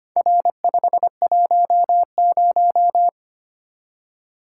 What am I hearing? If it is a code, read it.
Morse code, heard as a keyed tone, about 25 wpm, 700 Hz. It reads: R510
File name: practice / 25wpm / Four Letter-Number Combinations - 25wpm